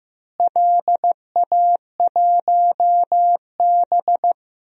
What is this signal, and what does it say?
Morse code, heard as a keyed tone, about 15 wpm, 700 Hz. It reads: LA1B
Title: DXCC-100 Call Signs - 4 Characters - Repeated 2x - 15wpm